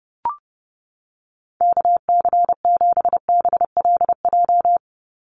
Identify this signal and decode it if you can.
Morse code, heard as a keyed tone, about 30 wpm, 700 Hz. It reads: KC7BLJ
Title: USA Call Signs 30wpm